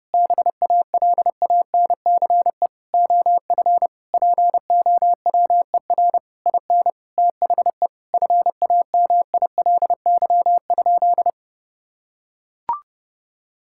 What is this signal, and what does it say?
Morse code, heard as a keyed tone, about 30 wpm, 700 Hz. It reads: BALANCE OF POWER IN THE FAMILY?
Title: Sentences from Top 1000 Words - Review Part 6 - 30wpm